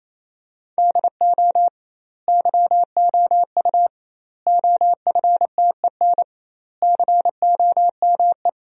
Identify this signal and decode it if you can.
Morse code, heard as a keyed tone, about 27 wpm, 700 Hz. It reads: DO YOU OFTEN COME
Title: 2x Word-Spacing - Sentences from Top 300 Words - 28wpm